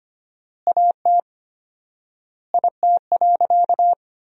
Morse code, heard as a keyed tone, about 25 wpm, 700 Hz. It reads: AT IT.
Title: 4x Word-Spacing - Sentences from Top 300 Words - 25wpm